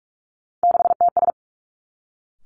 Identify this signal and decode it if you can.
Morse code, heard as a keyed tone, about 46 wpm, 700 Hz. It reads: 6TS
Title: Three Challenging Letter-Number Combinations 45wpm